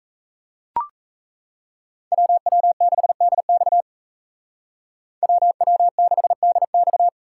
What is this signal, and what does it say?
Morse code, heard as a keyed tone, about 41 wpm, 700 Hz. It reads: WW6DX WW6DX